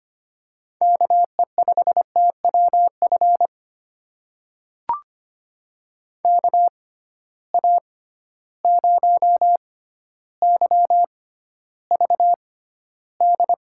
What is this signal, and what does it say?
Morse code, heard as a keyed tone, about 25 wpm, spaced as for 13 wpm, 700 Hz. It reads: KE5TWF KA0YVD
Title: USA Call Signs - 6 Characters - 6x Character Spacing - 25wpm